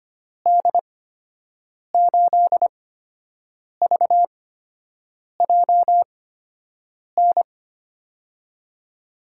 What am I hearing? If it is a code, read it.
Morse code, heard as a keyed tone, about 25 wpm, 700 Hz. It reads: D 8 V J N
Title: USA Call Signs - 8x Character Spacing - 25wpm